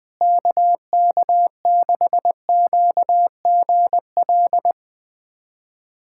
Morse code, heard as a keyed tone, about 21 wpm, 700 Hz. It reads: KK6QGL
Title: USA Callsign DE Callsign 20wpm